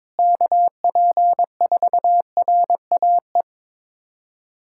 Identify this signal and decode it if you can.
Morse code, heard as a keyed tone, about 22 wpm, 700 Hz. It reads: KP4RAE
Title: USA Callsign DE Callsign - 6 Characters - 22wpm